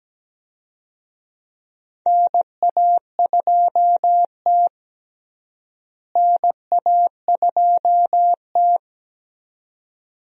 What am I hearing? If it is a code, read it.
Morse code, heard as a keyed tone, about 17 wpm, 700 Hz. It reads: NA2T NA2T
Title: USA Call Signs - 4 Characters - Repeated 2x - 17wpm